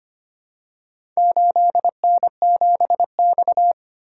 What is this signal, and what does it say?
Morse code, heard as a keyed tone, about 25 wpm, 700 Hz. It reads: 8N7X
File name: practice / 25wpm / Four Letter-Number Combinations - 25wpm